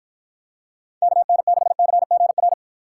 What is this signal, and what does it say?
Morse code, heard as a keyed tone, about 52 wpm, 700 Hz. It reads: KN6CGR